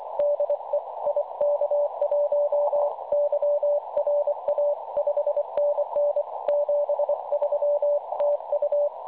Morse code, heard as a keyed tone, about 25 wpm, 560 Hz, with radio noise. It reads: DE IK1YRA 5NN 73 TU
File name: s500e 3523.9kHz